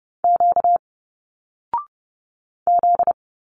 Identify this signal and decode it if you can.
Morse code, heard as a keyed tone, about 30 wpm, 700 Hz. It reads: Q Z
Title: Single Letters 30wpm